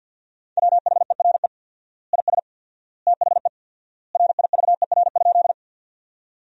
Morse code, heard as a keyed tone, about 50 wpm, 700 Hz. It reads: WHERE IS THE RIVER?
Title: 4x Word-Spacing - Sentences from Top 300 Words - 50wpm